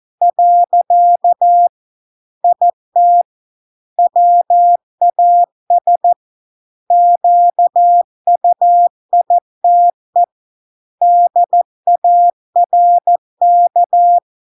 Morse code, heard as a keyed tone, about 14 wpm, 700 Hz. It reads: . IT WAS QUITE DARK